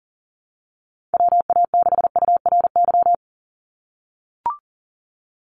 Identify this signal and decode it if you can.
Morse code, heard as a keyed tone, about 40 wpm, 700 Hz. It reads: WA6URY
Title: Call Signs - 40wpm